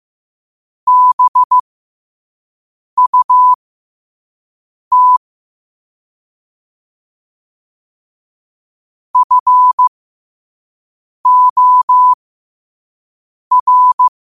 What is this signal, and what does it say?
Morse code, heard as a keyed tone, about 15 wpm, spaced as for 5 wpm, 1000 Hz. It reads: BUT FOR
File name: qotdfiles / 5wpm